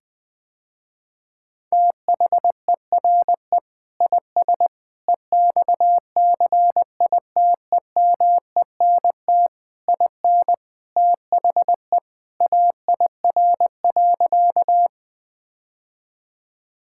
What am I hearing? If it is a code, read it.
Morse code, heard as a keyed tone, about 21 wpm, 700 Hz. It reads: THERE IS EXCITEMENT IN THE AIR.
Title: Sentences with 3-Word Phrases - Spoken Beginning - 20wpm